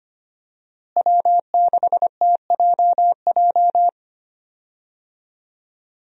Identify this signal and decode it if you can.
Morse code, heard as a keyed tone, about 25 wpm, 700 Hz. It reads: W6TJJ
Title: USA Call Signs - 5 Characters - Repeated 3x - 25wpm